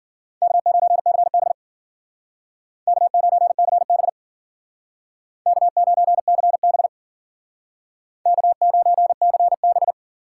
Morse code, heard as a keyed tone, about 50 wpm, 700 Hz. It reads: K9CB K9CB K9CB K9CB